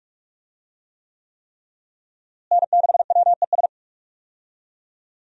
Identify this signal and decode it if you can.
Morse code, heard as a keyed tone, about 46 wpm, 700 Hz. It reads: N6WES
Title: USA Call Signs - 5 Characters - Repeated 3x - 45wpm